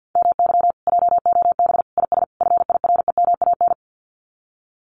Morse code, heard as a keyed tone, about 50 wpm, 700 Hz. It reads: MY JOB IS FIREMAN
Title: Common QSO Phrases Encore 50wpm